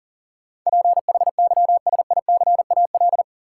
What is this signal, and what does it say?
Morse code, heard as a keyed tone, about 40 wpm, 700 Hz. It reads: PHYSICAL
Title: Sentences from Top 1000 Words - Review Part 4 - New Word Drill - 40wpm